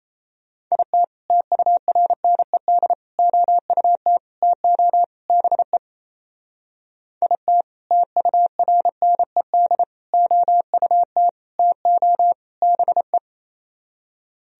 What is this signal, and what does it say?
Morse code, heard as a keyed tone, about 36 wpm, 700 Hz. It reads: IT TURNED OUT TO BE IT TURNED OUT TO BE